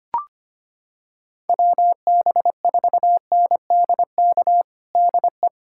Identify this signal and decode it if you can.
Morse code, heard as a keyed tone, about 25 wpm, 700 Hz. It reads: WB4NDK DE
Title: USA Callsign DE Callsign - 6 Characters - 25wpm